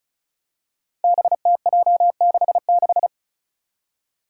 Morse code, heard as a keyed tone, about 34 wpm, 700 Hz. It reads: BTJ66
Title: Five Challenging Letter-Number Combinations 35wpm